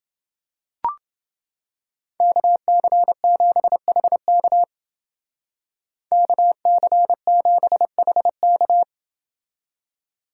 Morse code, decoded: KC7HK KC7HK

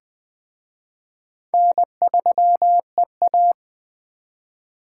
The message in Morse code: N3EA

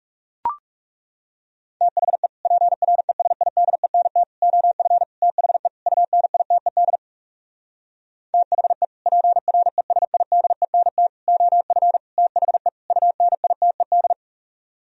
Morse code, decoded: THE PRESIDENT OF THE UNITED THE PRESIDENT OF THE UNITED

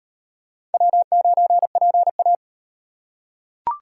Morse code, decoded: W9PA

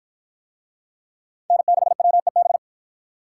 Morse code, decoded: N6WEB

